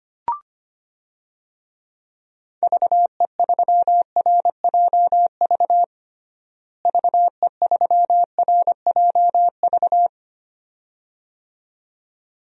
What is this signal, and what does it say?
Morse code, heard as a keyed tone, about 25 wpm, 700 Hz. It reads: VE3RJV VE3RJV
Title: DXCC-100 Call Signs - 6 Characters - Repeated 2x - 25wpm